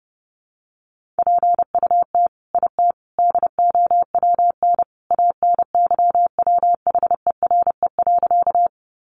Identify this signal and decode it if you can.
Morse code, heard as a keyed tone, about 30 wpm, 700 Hz. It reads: PUT IT DOWN ANYWHERE.
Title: Sentences from Top 1400 Words 30wpm